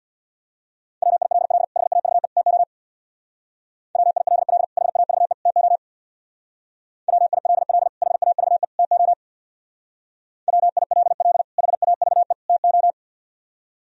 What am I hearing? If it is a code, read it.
Morse code, heard as a keyed tone, about 62 wpm, 700 Hz. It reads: WILL HAVE TO WILL HAVE TO WILL HAVE TO WILL HAVE TO
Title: Speed-Racing - Sets of 3 Words 50wpm